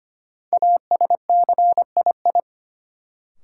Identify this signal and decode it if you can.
Morse code, heard as a keyed tone, about 25 wpm, 700 Hz. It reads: ASCII
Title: Sets of 5 Words - New Word Drill - 25wpm